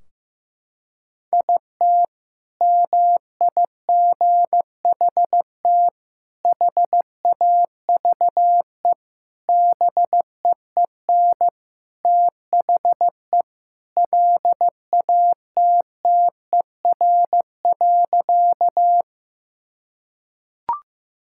Morse code, heard as a keyed tone, about 15 wpm, 700 Hz. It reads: IT MIGHT HAVE BEEN THE LATTER.
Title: Sentences from Top 1600 Words 15wpm